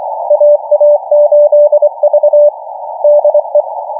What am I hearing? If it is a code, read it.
Morse code, heard as a keyed tone, about 24 wpm, 590 Hz. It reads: AA8V DE